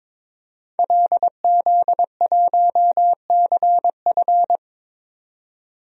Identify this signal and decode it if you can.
Morse code, heard as a keyed tone, about 22 wpm, 700 Hz. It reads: LZ1CF
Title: DXCC-100 Call Signs - 5 Characters - 22wpm